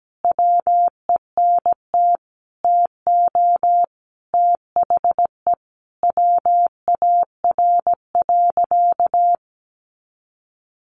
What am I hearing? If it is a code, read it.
Morse code, heard as a keyed tone, about 17 wpm, 700 Hz. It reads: WENT TO THE WAR.
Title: Sentences from Top 400 Words 17wpm